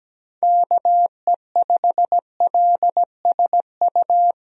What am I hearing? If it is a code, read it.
Morse code, heard as a keyed tone, about 17 wpm, 700 Hz. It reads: KE5LSU
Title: USA Call Signs - 6 Characters - Repeated 2x - 17wpm